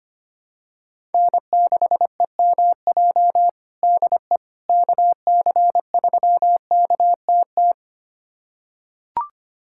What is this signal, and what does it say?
Morse code, heard as a keyed tone, about 25 wpm, 700 Hz. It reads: N6EMJ DE KC3KTT
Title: USA Callsign DE Callsign 25wpm